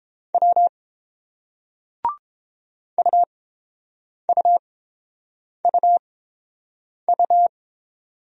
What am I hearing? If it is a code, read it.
Morse code, heard as a keyed tone, about 30 wpm, spaced as for 7 wpm, 700 Hz. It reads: W UUUU